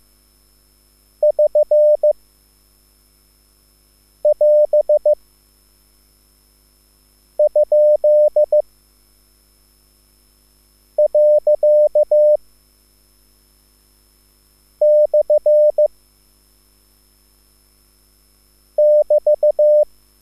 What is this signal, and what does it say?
Morse code, heard as a keyed tone, about 15 wpm, 600 Hz. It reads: <SN> <AS> ? . / <BT>
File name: QSL / CW lesson